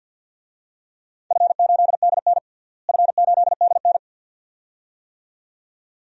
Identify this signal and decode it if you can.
Morse code, heard as a keyed tone, about 47 wpm, 700 Hz. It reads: F8DN F8DN
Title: DXCC-100 Call Signs - 4 Characters - Repeated 2x - 50wpm